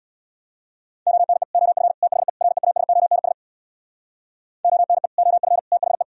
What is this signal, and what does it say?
Morse code, heard as a keyed tone, about 64 wpm, 700 Hz. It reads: ONE OF THE REASONS ONE OF THE